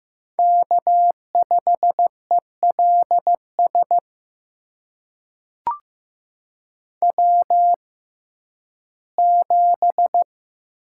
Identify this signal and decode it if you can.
Morse code, heard as a keyed tone, about 15 wpm, spaced as for 11 wpm, 700 Hz. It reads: K5ELS W7
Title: USA Call Signs - 5 Characters - 6x Character Spacing - 15wpm